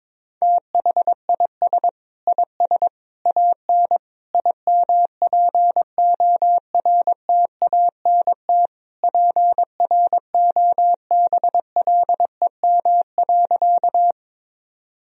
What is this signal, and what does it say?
Morse code, heard as a keyed tone, about 22 wpm, 700 Hz. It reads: THIS IS AN IMPORTANT PROBLEM.
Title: Sentences from Top 1000 Words - Review Part 2 - 22wpm